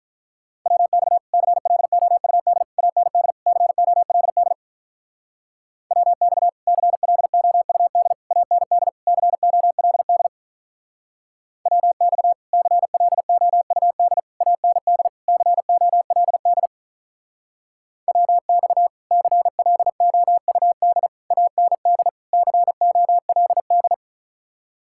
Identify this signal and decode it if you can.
Morse code, heard as a keyed tone, about 45 wpm, 700 Hz. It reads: WX CLOUD AND COLD WX CLOUD AND COLD WX CLOUD AND COLD WX CLOUD AND COLD